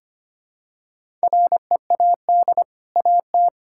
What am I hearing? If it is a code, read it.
Morse code, heard as a keyed tone, about 25 wpm, 700 Hz. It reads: READ AT
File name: Easy Sentences with Hints - 25wpm